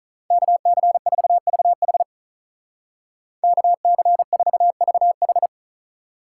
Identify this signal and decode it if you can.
Morse code, heard as a keyed tone, about 43 wpm, 700 Hz. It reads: KC4VH KC4VH
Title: Speed Racing - USA 2x2 Call Signs 35wpm